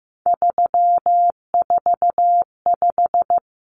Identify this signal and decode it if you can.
Morse code, heard as a keyed tone, about 15 wpm, 700 Hz. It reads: 345
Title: Numbers - 3-Digit 15wpm